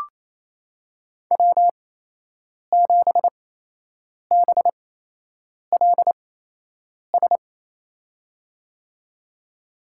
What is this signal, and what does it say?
Morse code, heard as a keyed tone, about 28 wpm, spaced as for 8 wpm, 700 Hz. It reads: W7BLS
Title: USA Call Signs - 8x Character Spacing - 28wpm